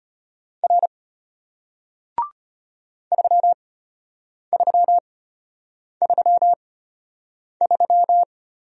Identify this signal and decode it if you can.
Morse code, heard as a keyed tone, about 33 wpm, spaced as for 8 wpm, 700 Hz. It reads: R 3333